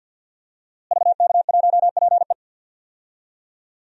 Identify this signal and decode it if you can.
Morse code, heard as a keyed tone, about 50 wpm, 700 Hz. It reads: VK1PE
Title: DXCC-100 Call Signs - 5 Characters - Repeated 3x - 50wpm